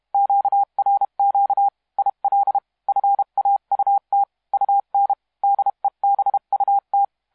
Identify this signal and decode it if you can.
Morse code, heard as a keyed tone, about 31 wpm, 800 Hz. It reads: QRQ IL FAUT UN DEBUT
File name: musiqueradio / cw hsc demo f6ddr